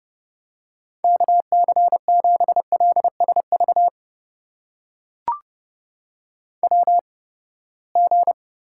Morse code, heard as a keyed tone, about 30 wpm, spaced as for 21 wpm, 700 Hz. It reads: KC7LSV WG
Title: USA Call Signs - 8x Character Spacing - 30wpm